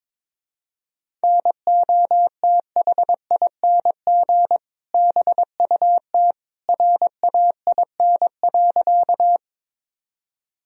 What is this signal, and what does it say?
Morse code, heard as a keyed tone, about 22 wpm, 700 Hz. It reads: NOTHING BUT RAIN.
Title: Sentences from Top 1000 Words - Review Part 1 - 22wpm